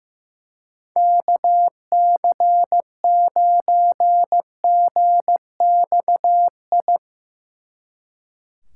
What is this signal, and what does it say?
Morse code, heard as a keyed tone, about 17 wpm, 700 Hz. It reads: KC9GXI
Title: Speed Racing - USA Call Signs 15wpm